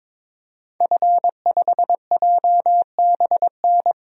Morse code, heard as a keyed tone, about 21 wpm, 700 Hz. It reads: F5JBN
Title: DXCC-100 Call Signs - 5 Characters - 22wpm